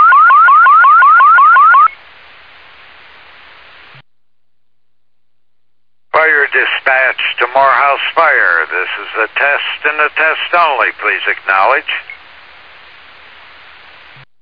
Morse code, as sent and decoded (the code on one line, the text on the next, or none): none